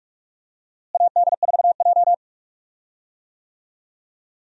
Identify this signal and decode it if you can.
Morse code, heard as a keyed tone, about 46 wpm, 700 Hz. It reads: AD4J
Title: USA Call Signs - 4 Characters - Repeated 2x - 45wpm